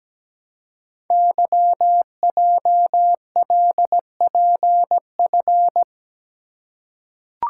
Morse code, heard as a keyed tone, about 17 wpm, 700 Hz. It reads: YJLPF
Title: Code Groups - 5 Letters - 17wpm